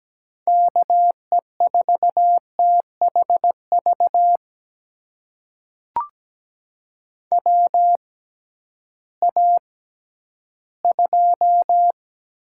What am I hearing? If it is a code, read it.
Morse code, heard as a keyed tone, about 17 wpm, spaced as for 11 wpm, 700 Hz. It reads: KE4THV WA2
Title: USA Call Signs - 6 Characters - 6x Character Spacing - 17wpm